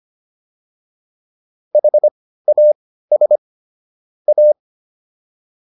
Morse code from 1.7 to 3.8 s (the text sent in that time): HAS